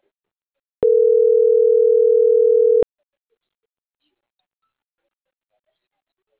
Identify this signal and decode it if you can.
Ringback tone (North America), 440 + 480 Hz.
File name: CENTRAL SAGOT / out xquenda.sagot